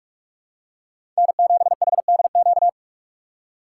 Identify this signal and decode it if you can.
Morse code, heard as a keyed tone, about 46 wpm, 700 Hz. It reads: N7HDQ
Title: USA Call Signs - 5 Characters - 2x Character Spacing - 45wpm